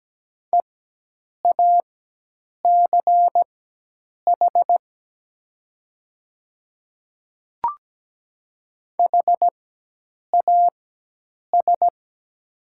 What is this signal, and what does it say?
Morse code, heard as a keyed tone, about 17 wpm, spaced as for 8 wpm, 700 Hz. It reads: EACH HAS
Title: 4x Character Spacing - Top 100 Words - 17wpm